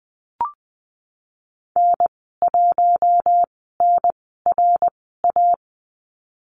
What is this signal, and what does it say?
Morse code, heard as a keyed tone, about 20 wpm, spaced as for 15 wpm, 700 Hz. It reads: N1NRA
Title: 082 Lesson 14g - Number 2 - Call Signs 20wpm